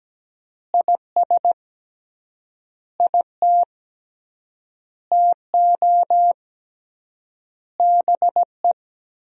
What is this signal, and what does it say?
Morse code, heard as a keyed tone, about 17 wpm, 700 Hz. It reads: IS IT TO BE
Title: 3x Word-Spacing - Sentences from Top 300 Words - 17wpm